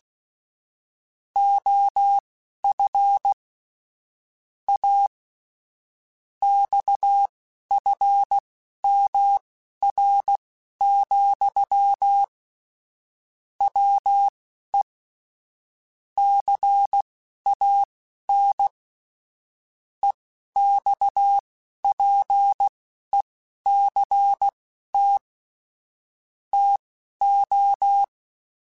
Morse code, heard as a keyed tone, about 16 wpm, spaced as for 11 wpm, 800 Hz. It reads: OF A XFMR, WE CAN EXPECT TO
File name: Technical Subjects / Technical Subjects-mp3s / Sw Pwr Sup @10